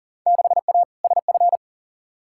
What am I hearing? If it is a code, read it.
Morse code, heard as a keyed tone, about 40 wpm, 700 Hz. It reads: 6A SF
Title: Field Day Exchanges - 40wpm